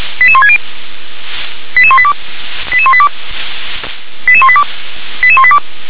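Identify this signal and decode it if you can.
Five-tone selective call (ZVEI), heard as quick runs of five tones, tones of about 70 ms. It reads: 88160 88182 88182 88182 88182
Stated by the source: ZVEI five-tone selective call received by SDR.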